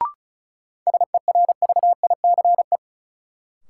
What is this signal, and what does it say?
Morse code, heard as a keyed tone, about 36 wpm, 700 Hz. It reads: SERVICE
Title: Sentences from Top 700 Words - New Word Drill - 35wpm